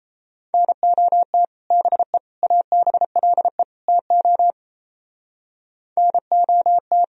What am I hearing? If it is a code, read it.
Morse code, heard as a keyed tone, about 35 wpm, 700 Hz. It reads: NOT BE ABLE TO NOT